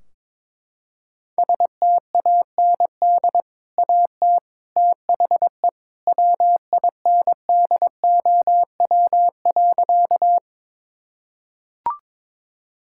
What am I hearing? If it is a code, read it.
Morse code, heard as a keyed tone, about 21 wpm, 700 Hz. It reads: STAND AT THE WINDOW.
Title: Sentences from Top 1600 Words 22wpm